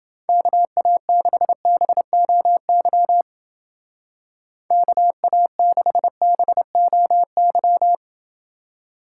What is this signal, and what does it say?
Morse code, heard as a keyed tone, about 28 wpm, 700 Hz. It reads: KA6BOY KA6BOY